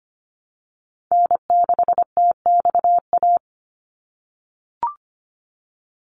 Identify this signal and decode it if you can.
Morse code, heard as a keyed tone, about 25 wpm, 690 Hz. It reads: N6TXA
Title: USA Call Signs 25wpm